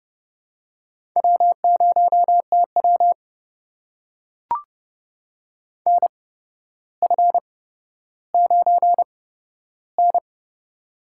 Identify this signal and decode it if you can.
Morse code, heard as a keyed tone, about 30 wpm, spaced as for 13 wpm, 700 Hz. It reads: W0TW NF9N
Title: USA Call Signs - 4 Characters - 8x Character Spacing - 30wpm